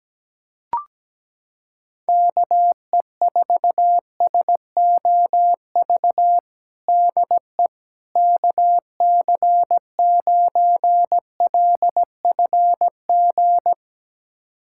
Morse code, heard as a keyed tone, about 17 wpm, 700 Hz. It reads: KE4SOV DE KC9LFG
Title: USA Callsign DE Callsign - 6 Characters - 17wpm